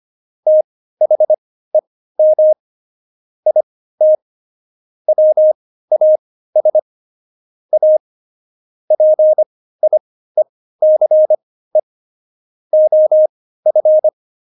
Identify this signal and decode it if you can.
Morse code, heard as a keyed tone, about 25 wpm, spaced as for 15 wpm, 620 Hz. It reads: THEM IT WAS A PIECE OF
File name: Jokes-25-char-15-eff